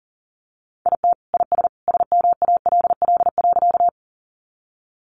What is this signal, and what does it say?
Morse code, heard as a keyed tone, about 40 wpm, 700 Hz. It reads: IT IS SMALL.
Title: Additional Easy Phrases 40wpm